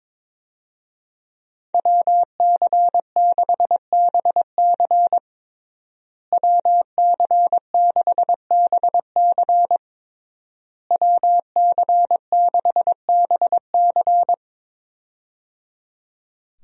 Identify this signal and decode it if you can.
Morse code, heard as a keyed tone, about 22 wpm, 700 Hz. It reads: WC6BC WC6BC WC6BC
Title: USA Call Signs - 5 Characters - Repeated 3x - 22wpm